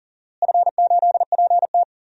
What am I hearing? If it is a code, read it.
Morse code, heard as a keyed tone, about 40 wpm, 700 Hz. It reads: F8PT
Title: DXCC-100 Call Signs - 4 Characters - Repeated 3x - 40wpm